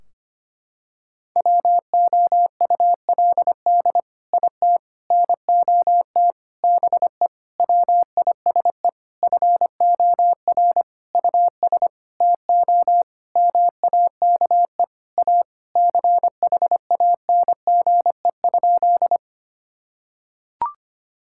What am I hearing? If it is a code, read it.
Morse code, heard as a keyed tone, about 25 wpm, 700 Hz. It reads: WOULD IT NOT BE WISE FOR US TO MAKE A CHANGE?
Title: Sentences from Top 1600 Words 25wpm